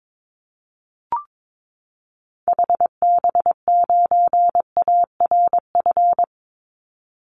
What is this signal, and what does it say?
Morse code, heard as a keyed tone, about 22 wpm, 700 Hz. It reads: HB9ARF
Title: Call Signs Repeated 3x - 22wpm